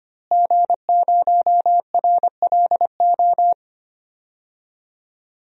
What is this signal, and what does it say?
Morse code, heard as a keyed tone, about 25 wpm, 700 Hz. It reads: G0RLO
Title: DXCC-100 Call Signs - 5 Characters - Repeated 3x - 25wpm